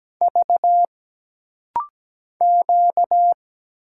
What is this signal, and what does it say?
Morse code, heard as a keyed tone, about 17 wpm, 700 Hz. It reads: V Q
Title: Single Letters - Rapid-Fire 17wpm